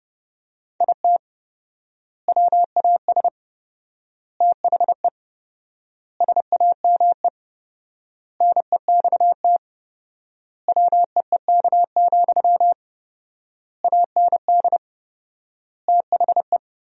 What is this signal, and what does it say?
Morse code, heard as a keyed tone, about 30 wpm, 700 Hz. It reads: IT WAS THE SAME NEXT WEEK, AND THE
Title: 4x Word-Spacing - Sentences from Top 500 Words - 30wpm